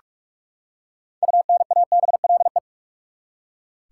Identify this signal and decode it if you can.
Morse code, heard as a keyed tone, about 42 wpm, 700 Hz. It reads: UNABLE
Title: Six Letter Words - 45wpm